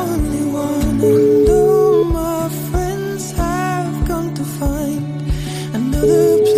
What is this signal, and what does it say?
Ringback tone (Europe, 425 Hz), with noise.